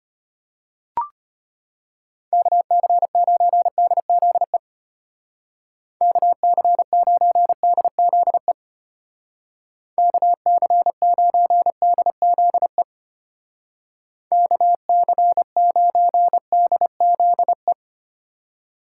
Morse code, heard as a keyed tone, about 32 wpm, 700 Hz. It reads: KC9DZE KC9DZE KC9DZE KC9DZE